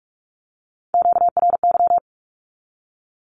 Morse code, heard as a keyed tone, about 44 wpm, 700 Hz. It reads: QRY
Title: Q-Codes 45wpm